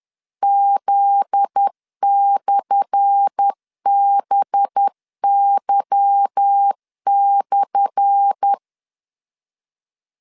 Morse code, heard as a keyed tone, about 10 wpm, 800 Hz. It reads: Z/BY/
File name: LEZIONI / 10 SIGNS / 10A SLASH 10wpm 2808